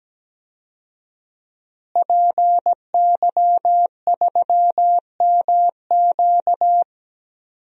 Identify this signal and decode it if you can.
Morse code, heard as a keyed tone, about 17 wpm, 700 Hz. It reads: PY3MQ
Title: DXCC-100 Call Signs - 5 Characters - Repeated 2x - 17wpm